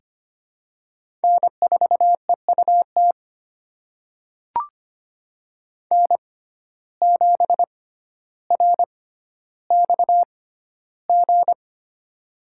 Morse code, heard as a keyed tone, about 25 wpm, spaced as for 13 wpm, 700 Hz. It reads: N4EUT N7RXG
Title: USA Call Signs - 5 Characters - 6x Character Spacing - 25wpm